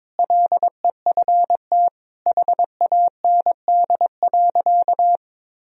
Morse code, heard as a keyed tone, about 22 wpm, 700 Hz. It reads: LEFT HAND.
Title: Sentences from Top 1000 Words - Review Part 2 - 22wpm